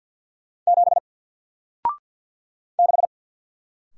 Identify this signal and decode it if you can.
Morse code, heard as a keyed tone, about 50 wpm, 700 Hz. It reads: 7 6
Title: Single Letter-Number Alt 50wpm